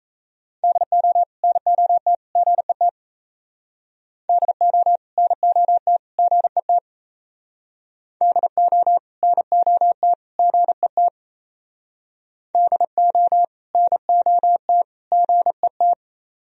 Morse code, heard as a keyed tone, about 37 wpm, 700 Hz. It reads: DO NOT GET DO NOT GET DO NOT GET DO NOT GET